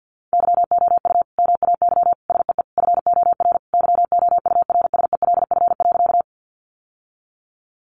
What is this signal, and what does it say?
Morse code, heard as a keyed tone, about 50 wpm, 700 Hz. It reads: YOU MAY SEE FOR YOURSELF.
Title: Sentences from Top 600 Words 50wpm